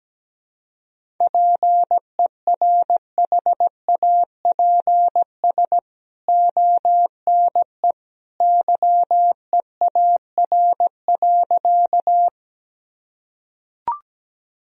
Morse code, heard as a keyed tone, about 17 wpm, 700 Hz. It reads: PERHAPS ONE YEAR.
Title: Sentences from Top 1000 Words - Review Part 1 - 17wpm